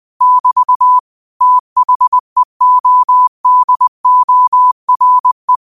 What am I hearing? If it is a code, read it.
Morse code, heard as a keyed tone, about 20 wpm, 1000 Hz. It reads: <BT> THEODORE